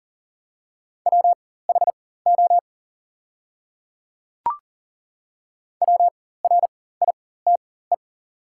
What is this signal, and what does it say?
Morse code, heard as a keyed tone, about 40 wpm, spaced as for 19 wpm, 700 Hz. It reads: WHO WRITE
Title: 4x Character Spacing - Top 200 Words - 40wpm